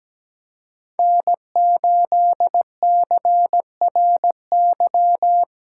Morse code, heard as a keyed tone, about 17 wpm, 700 Hz. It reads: N8CRY
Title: USA Call Signs - 5 Characters - 3x Character Spacing - 17wpm